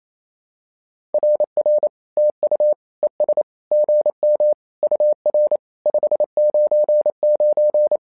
Morse code, heard as a keyed tone, about 27 wpm, 600 Hz. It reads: RR TU ES GM UR 599
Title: SOTA QSO Practice 28wpm